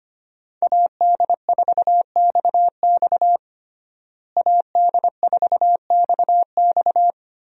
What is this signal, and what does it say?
Morse code, heard as a keyed tone, about 25 wpm, 700 Hz. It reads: AD4XX AD4XX
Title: USA Call Signs - 5 Characters - Repeated 3x - 25wpm